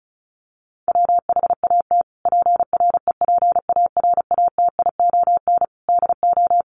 Morse code, heard as a keyed tone, about 34 wpm, 700 Hz. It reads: WHAT PREPARATION DO
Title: Sentences from Top 1400 Words 35wpm